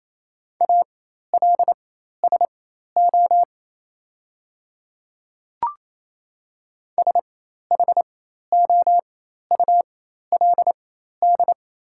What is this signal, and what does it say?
Morse code, heard as a keyed tone, about 28 wpm, spaced as for 13 wpm, 700 Hz. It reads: ALSO SHOULD
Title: 4x Character Spacing - Top 500 Words - 28wpm